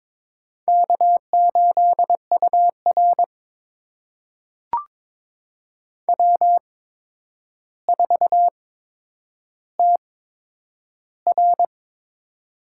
Morse code, heard as a keyed tone, about 22 wpm, spaced as for 9 wpm, 700 Hz. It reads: K8UR W4TR
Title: USA Call Signs - 8x Character Spacing - 22wpm